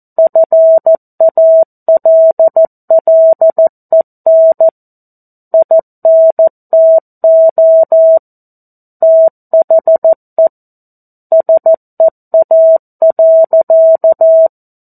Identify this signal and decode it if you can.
Morse code, heard as a keyed tone, about 14 wpm, 650 Hz. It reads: FALLEN INTO THE SEA.
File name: 02-Chapter-02 14 WPM